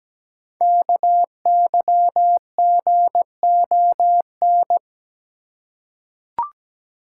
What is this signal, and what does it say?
Morse code, heard as a keyed tone, about 17 wpm, 700 Hz. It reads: KYGON
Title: Code Groups - 5 Letters - 17wpm